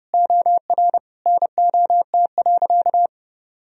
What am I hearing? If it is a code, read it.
Morse code, heard as a keyed tone, about 30 wpm, 700 Hz. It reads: OR NOT.
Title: Sentences from Top 1000 Words - Review Part 3 - 30wpm